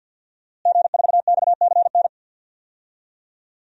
Morse code, heard as a keyed tone, about 50 wpm, 700 Hz. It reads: G4XCN
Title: DXCC-100 Call Signs - 5 Characters - Repeated 2x - 50wpm